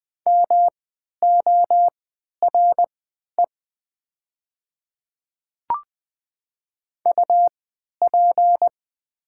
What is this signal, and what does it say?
Morse code, heard as a keyed tone, about 20 wpm, spaced as for 11 wpm, 700 Hz. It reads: MORE UP
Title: 3x Character Spacing - Top 100 Words - 20wpm